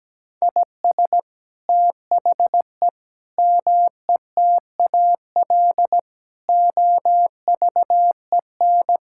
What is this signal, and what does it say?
Morse code, heard as a keyed tone, about 17 wpm, 700 Hz. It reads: IS THE METAL OVEN
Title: Sentences from Top 1000 Words - Review Part 6 - 17wpm